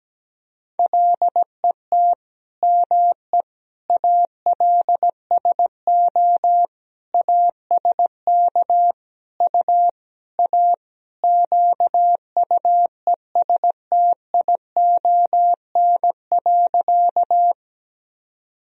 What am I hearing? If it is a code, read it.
Morse code, heard as a keyed tone, about 17 wpm, 700 Hz. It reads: LET ME ALSO ASK U A QUESTION.
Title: Sentences from Top 400 Words and Abbreviations 17wpm